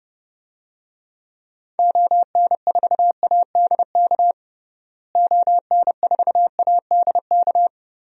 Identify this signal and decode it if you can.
Morse code, heard as a keyed tone, about 30 wpm, 700 Hz. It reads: ON4ADK ON4ADK
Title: DXCC-100 Call Signs - 6 Characters - Repeated 3x - 30wpm